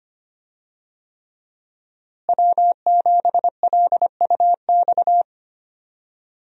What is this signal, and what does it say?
Morse code, heard as a keyed tone, about 25 wpm, 700 Hz. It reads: W7LUX
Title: USA Call Signs - 5 Characters - Repeated 3x - 25wpm